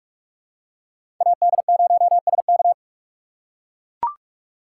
Morse code, heard as a keyed tone, about 44 wpm, 700 Hz. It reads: AD0SK
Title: USA Call Signs - 5 Characters - 3x Character Spacing - 45wpm